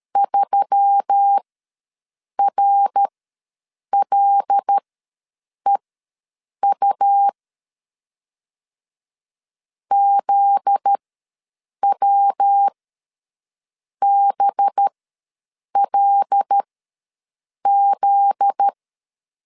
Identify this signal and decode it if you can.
Morse code, heard as a keyed tone, about 13 wpm, spaced as for 7 wpm, 800 Hz. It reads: 3RLEU ZWBLZ
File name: LEZIONI / 7 GF9Z / 7 GF9Z ws 12wpm 3321